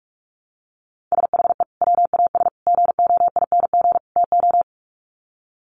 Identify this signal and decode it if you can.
Morse code, heard as a keyed tone, about 44 wpm, 700 Hz. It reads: SHE WAS GOING TO
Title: Sets of 4 Words 45wpm